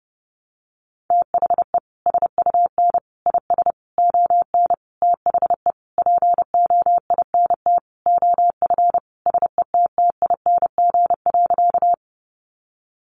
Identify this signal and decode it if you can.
Morse code, heard as a keyed tone, about 30 wpm, 700 Hz. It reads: THE SUN IS ON THE POINT OF SETTING.
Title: Sentences from Top 1300 Words 30wpm